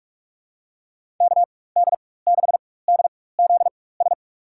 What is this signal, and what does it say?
Morse code, heard as a keyed tone, about 46 wpm, spaced as for 21 wpm, 700 Hz. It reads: KD6DZS
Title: USA Call Signs - 6 Characters - 4x Character Spacing - 45wpm